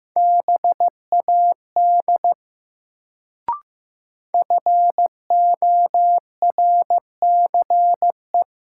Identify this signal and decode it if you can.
Morse code, heard as a keyed tone, about 15 wpm, 700 Hz. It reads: BAD FORCE
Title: Sentences from Top 700 Words - New Word Drill - 15wpm